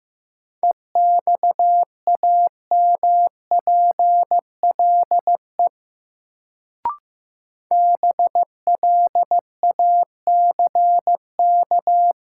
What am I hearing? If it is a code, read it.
Morse code, heard as a keyed tone, about 15 wpm, 700 Hz. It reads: EXAMPLE BLACK
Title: Sentences from Top 400 Words - New Word Drill - 15wpm